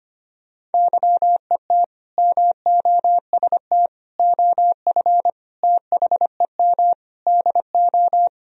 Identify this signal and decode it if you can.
Morse code, heard as a keyed tone, about 25 wpm, 700 Hz. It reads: YET MOST OF THEM DO
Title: Sentences from Top 1000 Words - Review Part 1 - 25wpm